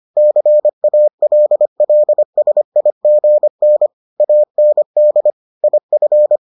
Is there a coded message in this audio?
Yes: Morse code from 0.2 to 6.4 s, heard as a keyed tone, about 25 wpm, 600 Hz. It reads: CALLSIGN AND IF